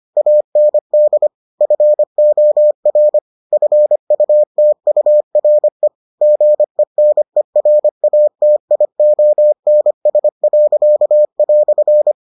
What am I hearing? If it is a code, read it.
Morse code, heard as a keyed tone, about 25 wpm, 600 Hz. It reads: AND FOR FUTURE GENERATIONS."